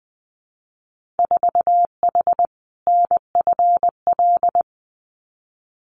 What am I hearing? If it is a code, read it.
Morse code, heard as a keyed tone, about 20 wpm, 700 Hz. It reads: 4H NFL
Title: Winter Field Day Exchanges Encore 20wpm